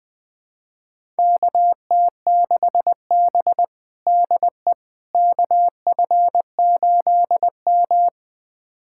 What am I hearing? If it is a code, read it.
Morse code, heard as a keyed tone, about 20 wpm, 700 Hz. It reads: KT6B DE KF8M
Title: USA Callsign DE Callsign - 4 Characters - 20wpm